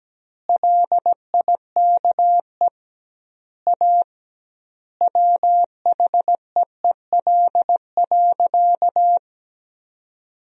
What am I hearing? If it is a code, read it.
Morse code, heard as a keyed tone, about 17 wpm, 700 Hz. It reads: LIKE A WHEEL.
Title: 2x Word-Spacing - Sentences from Top 500 Words - 17wpm